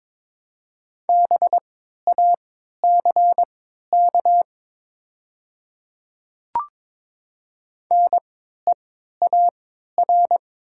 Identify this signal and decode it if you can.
Morse code, heard as a keyed tone, about 22 wpm, spaced as for 12 wpm, 700 Hz. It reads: BACK NEAR
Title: 3x Character Spacing - Top 200 Words - 22wpm